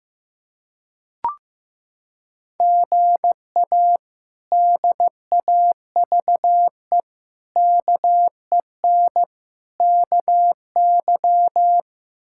Morse code, heard as a keyed tone, about 15 wpm, 700 Hz. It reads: GA DAVE KEN KY
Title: K1USN SST QSO Practice - 15wpm